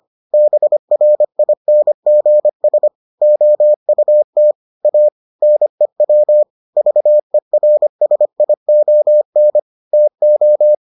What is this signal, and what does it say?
Morse code, heard as a keyed tone, about 25 wpm, 600 Hz. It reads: BRINGS OUT A NEW VERSION TO